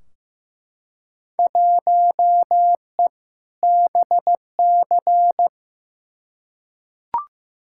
Morse code, heard as a keyed tone, about 15 wpm, 700 Hz. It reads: 1E BC
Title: Field Day Exchanges - 15wpm